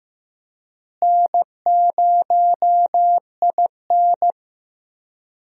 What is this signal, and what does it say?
Morse code, heard as a keyed tone, about 15 wpm, 700 Hz. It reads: N0IN
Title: USA Call Signs - 4 Characters - Repeated 2x - 15wpm